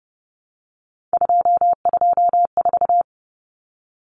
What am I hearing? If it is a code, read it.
Morse code, heard as a keyed tone, about 30 wpm, 700 Hz. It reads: 224
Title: Common Cut Numbers - 3-Digit 30wpm